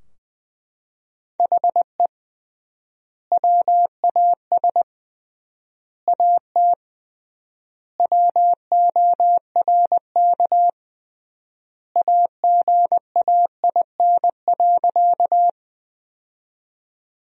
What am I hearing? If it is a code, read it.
Morse code, heard as a keyed tone, about 21 wpm, 700 Hz. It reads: HE WAS AT WORK AGAIN.
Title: 3x Word-Spacing - Sentences from Top 200 Words - 20wpm